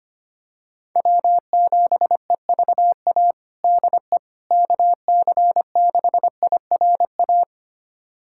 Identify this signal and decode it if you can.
Morse code, heard as a keyed tone, about 25 wpm, 700 Hz. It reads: W7EVA DE KC6IRA
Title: USA Callsign DE Callsign 25wpm